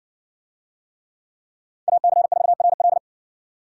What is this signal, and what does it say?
Morse code, heard as a keyed tone, about 58 wpm, 700 Hz. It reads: AC4RL